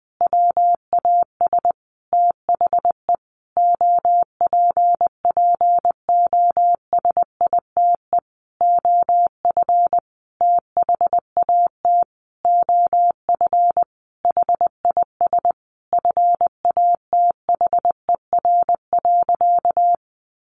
Morse code, heard as a keyed tone, about 20 wpm, 700 Hz. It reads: WAS THE OPPOSITE OF THAT OF HIS FATHER.
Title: Sentences from Top 1300 Words 20wpm